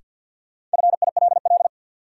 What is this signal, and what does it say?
Morse code, heard as a keyed tone, about 52 wpm, 700 Hz. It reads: FILL
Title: Sentences from Top 1000 Words - Review Part 3 - New Word Drill - 50wpm